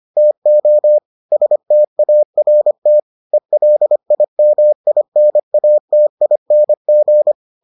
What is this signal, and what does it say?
Morse code, heard as a keyed tone, about 25 wpm, 600 Hz. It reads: TO START ELIMINATING